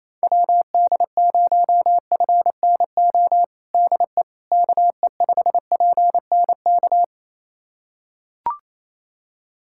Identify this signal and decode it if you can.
Morse code, heard as a keyed tone, about 27 wpm, 700 Hz. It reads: WD0FNO DE KE5PNK
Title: USA Callsign DE Callsign - 6 Characters - 28wpm